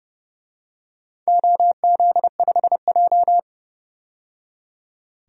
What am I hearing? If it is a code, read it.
Morse code, heard as a keyed tone, about 30 wpm, 700 Hz. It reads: OZ5J
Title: DXCC-100 Call Signs - 4 Characters - 30wpm